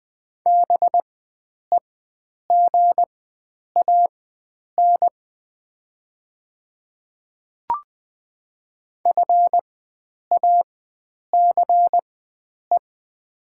Morse code, heard as a keyed tone, about 21 wpm, spaced as for 9 wpm, 700 Hz. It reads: BEGAN FACE
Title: 4x Character Spacing - Top 300 Words - 20wpm